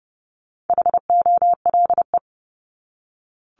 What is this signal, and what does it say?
Morse code, heard as a keyed tone, about 30 wpm, 700 Hz. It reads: HOLE
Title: Top 1501-1750 Words 30wpm